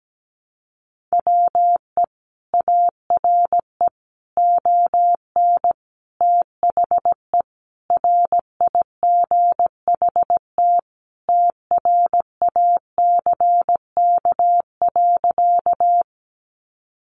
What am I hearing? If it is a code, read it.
Morse code, heard as a keyed tone, about 17 wpm, 700 Hz. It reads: WE ARE ON THE RIGHT TRACK.
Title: Sentences from Top 1100 Words 17wpm